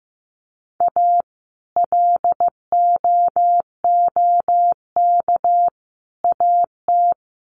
Morse code, heard as a keyed tone, about 15 wpm, 700 Hz. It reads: A LOOK AT
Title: Sets of 5 Words Encore2 15wpm